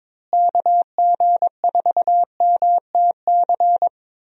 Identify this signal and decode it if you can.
Morse code, heard as a keyed tone, about 21 wpm, 700 Hz. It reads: KG4MTC
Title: USA Call Signs - 6 Characters - Repeated 3x - 22wpm